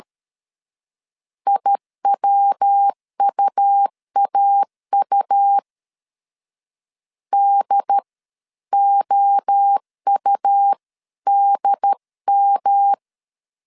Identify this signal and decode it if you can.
Morse code, heard as a keyed tone, about 13 wpm, spaced as for 11 wpm, 800 Hz. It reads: IWUAU DOUDM
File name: CW / DET / 1 3 1500C S B 12wpm 2828